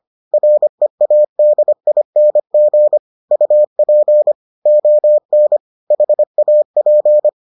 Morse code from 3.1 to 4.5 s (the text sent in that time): UP